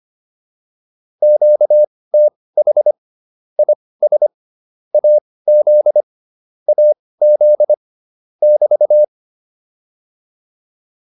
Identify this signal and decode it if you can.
Morse code, heard as a keyed tone, about 25 wpm, spaced as for 18 wpm, 600 Hz. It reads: QTH IS AZ AZ <BT>